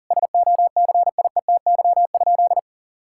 Morse code, heard as a keyed tone, about 40 wpm, 700 Hz. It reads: SOCIETY?
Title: Sentences from Top 1000 Words - Review Part 6 - 40wpm